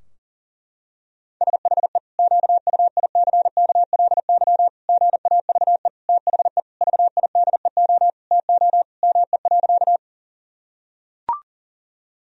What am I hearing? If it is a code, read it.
Morse code, heard as a keyed tone, about 40 wpm, 700 Hz. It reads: SHE QUICKLY GAVE THE VIDEO TO ME.
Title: Sentences from Top 1000 Words - Review Part 4 - 40wpm